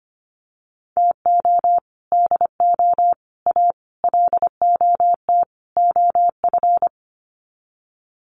Morse code, heard as a keyed tone, about 25 wpm, 700 Hz. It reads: TO DO A LOT OF
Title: Sets of 5 Words Encore 25wpm